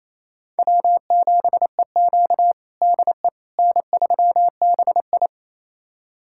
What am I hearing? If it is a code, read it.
Morse code, heard as a keyed tone, about 28 wpm, 700 Hz. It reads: W7EQ DE N3BI
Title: USA Callsign DE Callsign - 4 Characters - 28wpm